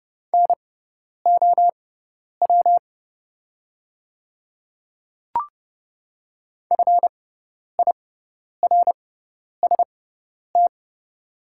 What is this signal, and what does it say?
Morse code, heard as a keyed tone, about 31 wpm, spaced as for 10 wpm, 700 Hz. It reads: NOW FIRST
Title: 6x Character Spacing - Top 200 Words - 30wpm